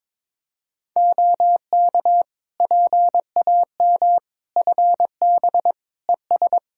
Morse code, heard as a keyed tone, about 22 wpm, 700 Hz. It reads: OK PAM FB ES
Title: K1USN SST QSO Practice 22wpm